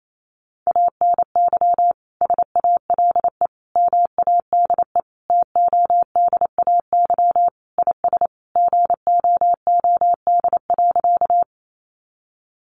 Morse code, heard as a keyed tone, about 28 wpm, 700 Hz. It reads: ANY SALE MADE TODAY IS GOOD.
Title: Sentences from Top 1400 Words 28wpm